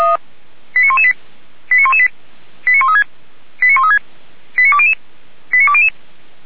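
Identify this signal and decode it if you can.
Five-tone selective call (ZVEI), heard as quick runs of five tones, tones of about 70 ms. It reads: 89118 89118 89136 89136 89200 89200
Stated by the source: Received by SDR.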